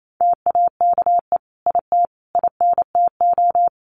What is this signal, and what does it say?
Morse code, heard as a keyed tone, about 28 wpm, 700 Hz. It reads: TAKE IT INTO
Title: Sentences from Top 1500 Words 28wpm